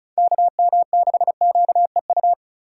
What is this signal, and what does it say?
Morse code, heard as a keyed tone, about 34 wpm, 700 Hz. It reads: KM6QEU
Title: USA Call Signs - 3x Character Spacing - 35wpm